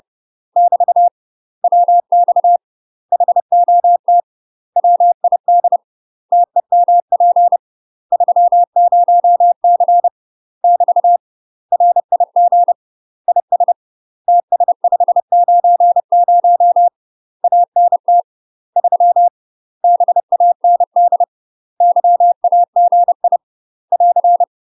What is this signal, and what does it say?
Morse code, heard as a keyed tone, about 30 wpm, 700 Hz. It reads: <BT> WX HOT WID TEMP 30C <BT> RIG IS TS590 ANT 3 BAND YAGI <AR>
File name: UK MASTER QSOs TRAINING / 30WPM / UK MASTER QSOs 0046